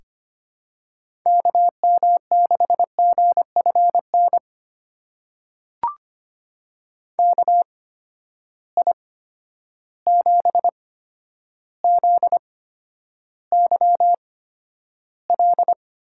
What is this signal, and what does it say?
Morse code, heard as a keyed tone, about 25 wpm, spaced as for 11 wpm, 700 Hz. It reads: KM6GFN KI7ZYL
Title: USA Call Signs - 6 Characters - 8x Character Spacing - 25wpm